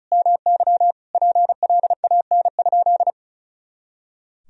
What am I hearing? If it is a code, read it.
Morse code, heard as a keyed tone, about 36 wpm, 700 Hz. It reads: MY PLAN?
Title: Sentences from Top 500 Words and Abbreviations 35wpm